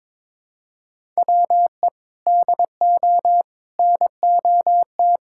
Morse code, heard as a keyed tone, about 22 wpm, 700 Hz. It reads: WE DO NOT
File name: practice / 22wpm / Sentences with 4-Word Phrases - 22wpm